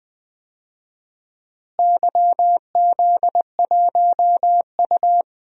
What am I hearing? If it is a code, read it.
Morse code, heard as a keyed tone, about 20 wpm, 700 Hz. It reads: YZ1U
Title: DXCC-100 Call Signs - 4 Characters - Repeated 2x - 20wpm